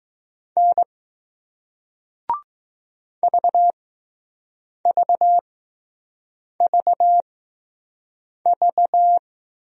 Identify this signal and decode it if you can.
Morse code, heard as a keyed tone, about 20 wpm, spaced as for 6 wpm, 700 Hz. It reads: N VVVV